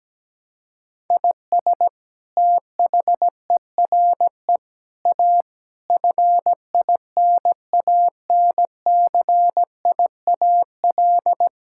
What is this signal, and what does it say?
Morse code, heard as a keyed tone, about 17 wpm, 700 Hz. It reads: IS THERE A FINANCIAL
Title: Sentences from Top 1000 Words - Review Part 5 - 17wpm